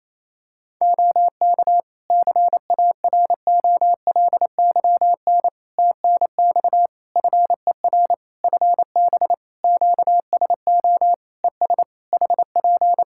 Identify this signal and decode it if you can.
Morse code, heard as a keyed tone, about 27 wpm, 700 Hz. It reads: OK CAROLYN TNX FER FB QSO ES HP
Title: Practice QSO - Exchange 4 - Ending - 28wpm